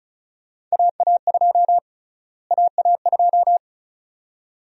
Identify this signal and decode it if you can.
Morse code, heard as a keyed tone, about 36 wpm, 700 Hz. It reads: AA2 AA2
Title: USA Call Sign Prefixes - Repeated 2x - 35wpm